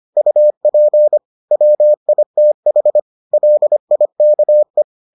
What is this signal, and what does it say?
Morse code, heard as a keyed tone, about 25 wpm, 600 Hz. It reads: UP WITH LIKE